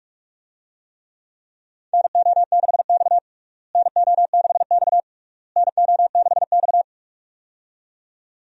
Morse code, heard as a keyed tone, about 46 wpm, 700 Hz. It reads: NO6X NO6X NO6X
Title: USA Call Signs - 4 Characters - Repeated 3x - 45wpm